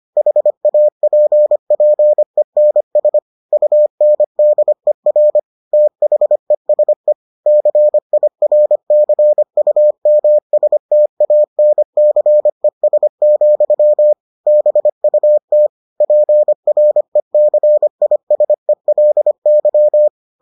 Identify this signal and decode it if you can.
Morse code, heard as a keyed tone, about 25 wpm, 600 Hz. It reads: HAPPENS UNDER THESE CIRCUMSTANCES, BUT PRECISELY